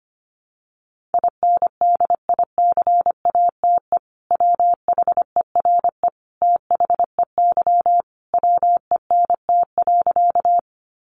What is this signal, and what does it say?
Morse code, heard as a keyed tone, about 25 wpm, 700 Hz. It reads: INDICATE WHERE THEY WENT.
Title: Sentences from Top 1300 Words 25wpm